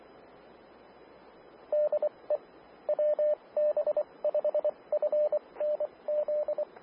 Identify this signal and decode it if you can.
Morse code, heard as a keyed tone, about 24 wpm, spaced as for 20 wpm, 600 Hz, with radio noise. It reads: DE WB5FNZ
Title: Austin, TX Amateur Repeaters